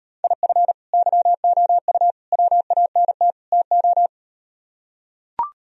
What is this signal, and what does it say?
Morse code, heard as a keyed tone, about 38 wpm, 700 Hz. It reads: IF YOU WANT TO